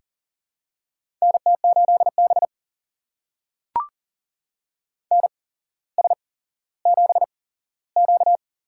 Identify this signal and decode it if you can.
Morse code, heard as a keyed tone, about 42 wpm, spaced as for 17 wpm, 700 Hz. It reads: NT8B NS7Q
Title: USA Call Signs - 4 Characters - 8x Character Spacing - 40wpm